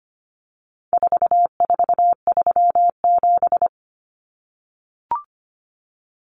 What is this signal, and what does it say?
Morse code, heard as a keyed tone, about 25 wpm, 700 Hz. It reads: 4437
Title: Numbers - 4-Digit 25wpm